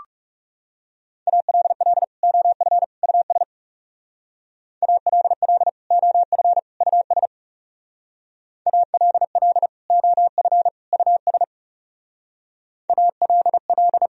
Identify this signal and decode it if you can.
Morse code, heard as a keyed tone, about 40 wpm, 700 Hz. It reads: ALL OF US ALL OF US ALL OF US ALL